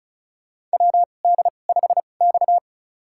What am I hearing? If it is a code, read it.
Morse code, heard as a keyed tone, about 36 wpm, spaced as for 25 wpm, 700 Hz. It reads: WD5X
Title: USA Call Signs - 4 Characters - 2x Character Spacing - 35wpm